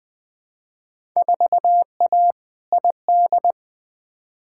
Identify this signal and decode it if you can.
Morse code, heard as a keyed tone, about 20 wpm, 700 Hz. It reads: 4A ID
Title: Field Day Exchanges Encore - 20wpm